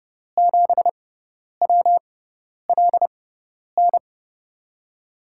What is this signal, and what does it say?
Morse code, heard as a keyed tone, about 31 wpm, spaced as for 10 wpm, 700 Hz. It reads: 7WLN
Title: USA Call Signs - 6 Characters - 6x Character Spacing - 30wpm